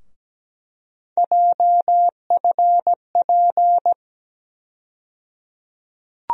Morse code, heard as a keyed tone, about 17 wpm, 700 Hz. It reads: JFP
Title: USA Call Sign Postfixes - 3 Letters 17wpm